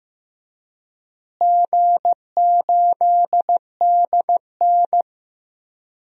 Morse code, heard as a keyed tone, about 15 wpm, 700 Hz. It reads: G8DN
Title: DXCC-100 Call Signs - 4 Characters - Repeated 3x - 15wpm